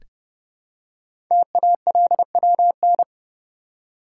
Morse code, heard as a keyed tone, about 30 wpm, 700 Hz. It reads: TALWN
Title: Code Groups - 5 Letters - 30wpm